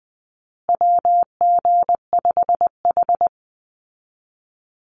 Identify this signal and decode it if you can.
Morse code, heard as a keyed tone, about 19 wpm, 700 Hz. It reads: WG5H
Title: Call Signs Repeated 3x - 20wpm